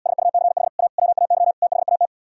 Morse code, heard as a keyed tone, about 77 wpm, 700 Hz. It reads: SUCH A WAY THAT